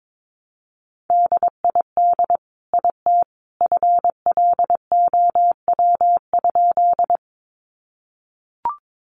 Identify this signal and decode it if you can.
Morse code, heard as a keyed tone, about 22 wpm, 700 Hz. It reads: DID IT FLOW?
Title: Sentences from Top 1200 Words 22wpm